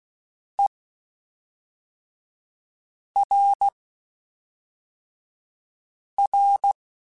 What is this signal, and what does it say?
Morse code, heard as a keyed tone, about 16 wpm, spaced as for 3 wpm, 800 Hz. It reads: ERR
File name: Spaced / Chuckles Spaced-mp3s / HumanErr sp @5